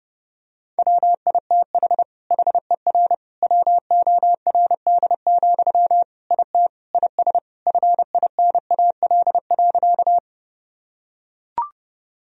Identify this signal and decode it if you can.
Morse code, heard as a keyed tone, about 30 wpm, 700 Hz. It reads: WITH HER WORD, IT IS FINAL.
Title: Sentences from Top 1000 Words - Review Part 2 - 30wpm